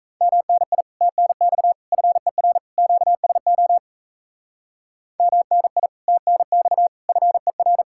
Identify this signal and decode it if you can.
Morse code, heard as a keyed tone, about 32 wpm, 700 Hz. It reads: MNI TNX FER QSO MNI TNX FER
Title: Speed-Racing Common QSO Phrases - 28wpm